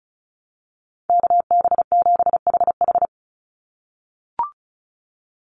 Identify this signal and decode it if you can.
Morse code, heard as a keyed tone, about 34 wpm, 700 Hz. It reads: KB7HH
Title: NA SOTA Chaser Callsigns 35wpm